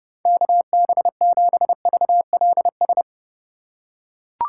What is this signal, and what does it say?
Morse code, heard as a keyed tone, about 32 wpm, 700 Hz. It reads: KB7VLS
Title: USA Call Signs - 6 Characters - 4x Character Spacing - 30wpm